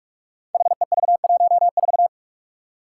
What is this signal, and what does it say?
Morse code, heard as a keyed tone, about 44 wpm, 700 Hz. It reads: HEV14
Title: Five Challenging Letter-Number Combinations 45wpm